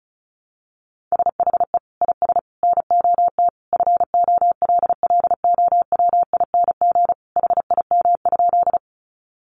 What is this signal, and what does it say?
Morse code, heard as a keyed tone, about 34 wpm, 700 Hz. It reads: SHE IS NOT FOLLOWING HIM?
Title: Sentences from Top 700 Words 35wpm